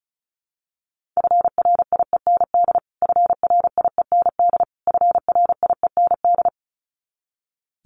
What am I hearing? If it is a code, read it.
Morse code, heard as a keyed tone, about 34 wpm, 700 Hz. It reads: FRIEND FRIEND FRIEND
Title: Top 500 Words Repeated 35wpm